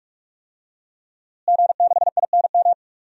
Morse code, heard as a keyed tone, about 44 wpm, 700 Hz. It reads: G6INM
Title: DXCC-100 Call Signs - 5 Characters - Repeated 2x - 45wpm